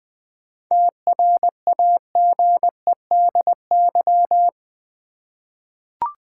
Morse code, heard as a keyed tone, about 20 wpm, 700 Hz. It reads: TRAGEDY
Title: 7 Letter Words - First 3 Letters Given - 20wpm